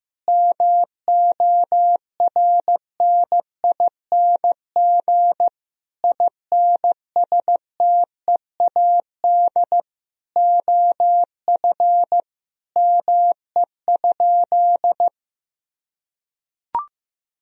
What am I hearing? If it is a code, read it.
Morse code, heard as a keyed tone, about 16 wpm, 700 Hz. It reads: MORNING INSTEAD OF ME?
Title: Sentences from Top 1000 Words - Review Part 4 - 15wpm